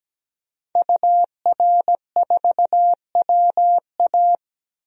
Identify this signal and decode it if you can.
Morse code, heard as a keyed tone, about 17 wpm, 700 Hz. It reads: UR4WA
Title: DXCC-100 Call Signs - 5 Characters - 17wpm